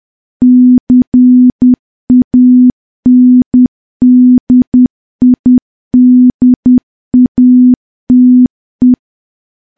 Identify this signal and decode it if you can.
Morse code, heard as a keyed tone, about 10 wpm, 250 Hz. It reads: CANDIDATE
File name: morse / morse-training-10WPM